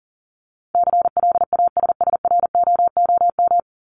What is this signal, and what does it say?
Morse code, heard as a keyed tone, about 38 wpm, 700 Hz. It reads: CLASSROOM
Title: Top 1251-1500 Words 40wpm